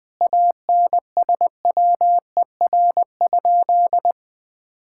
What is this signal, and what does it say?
Morse code, heard as a keyed tone, about 20 wpm, 700 Hz. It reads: ANSWER?
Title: Sentences from Top 1000 Words - Review Part 6 - 20wpm